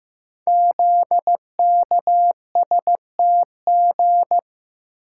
Morse code, heard as a keyed tone, about 15 wpm, 700 Hz. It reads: ZKSTG
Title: Code Groups - 5 Letters - 15wpm